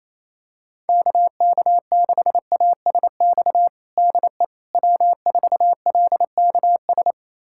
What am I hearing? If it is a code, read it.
Morse code, heard as a keyed tone, about 28 wpm, 700 Hz. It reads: KK6ASX DE W4LKS
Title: USA Callsign DE Callsign 28wpm